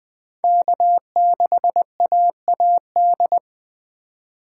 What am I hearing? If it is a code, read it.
Morse code, heard as a keyed tone, about 20 wpm, 700 Hz. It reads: K6AAD
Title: USA Call Signs - 5 Characters - 2x Character Spacing - 20wpm